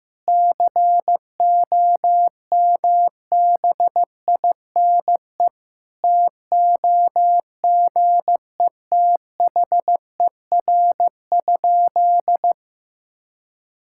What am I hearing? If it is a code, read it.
Morse code, heard as a keyed tone, about 15 wpm, 700 Hz. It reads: COMBINE TOGETHER?
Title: Sentences from Top 1600 Words 15wpm